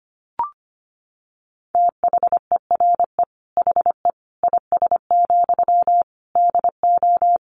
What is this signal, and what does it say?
Morse code, heard as a keyed tone, about 25 wpm, 700 Hz. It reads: THERE HE IS, DO
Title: Sentences from Top 100 Words 25wpm